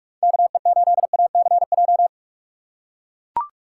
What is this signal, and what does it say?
Morse code, heard as a keyed tone, about 44 wpm, 700 Hz. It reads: KE8ACJ
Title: USA Call Signs - 6x Character Spacing - 45wpm